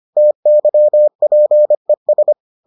Morse code, heard as a keyed tone, about 25 wpm, 600 Hz. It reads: TYPES